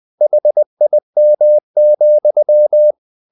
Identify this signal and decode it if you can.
Morse code, heard as a keyed tone, about 20 wpm, 600 Hz. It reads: HIM,
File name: the old man and the sea part 2 20wpm